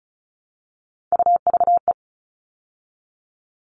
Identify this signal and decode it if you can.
Morse code, heard as a keyed tone, about 34 wpm, 700 Hz. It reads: UVE
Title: All Cut Numbers - 3-Digit 35wpm